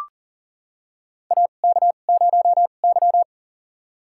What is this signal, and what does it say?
Morse code, heard as a keyed tone, about 40 wpm, spaced as for 29 wpm, 700 Hz. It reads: AK0Y
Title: USA Call Signs - 2x Character Spacing - 40wpm